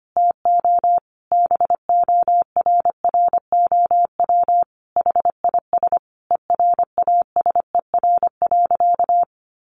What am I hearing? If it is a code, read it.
Morse code, heard as a keyed tone, about 25 wpm, 700 Hz. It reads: TO BORROW HIS ERASER.
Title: Additional Easy Phrases 25wpm